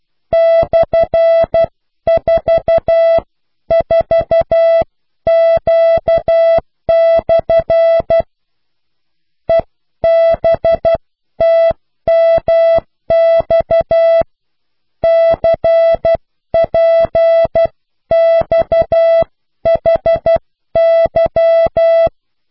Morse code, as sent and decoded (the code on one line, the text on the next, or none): -..-. ....- ....- --.- -..-. / . -... - -- -..- / -.-. .--. -..- .... -.--
/44Q/ EBTMX CPXHY